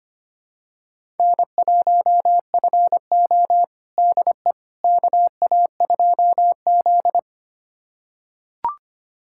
Morse code, heard as a keyed tone, about 25 wpm, 700 Hz. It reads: N1FO DE KA2Z
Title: USA Callsign DE Callsign - 4 Characters - 25wpm